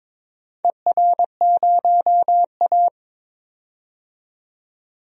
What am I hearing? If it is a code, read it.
Morse code, heard as a keyed tone, about 21 wpm, 700 Hz. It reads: ER0A
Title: DXCC-100 Call Signs - 4 Characters - 22wpm